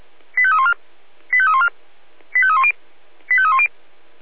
Five-tone selective call (ZVEI), heard as quick runs of five tones, tones of about 70 ms. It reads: 85314 85314 85310 85310
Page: SDR/ZVEI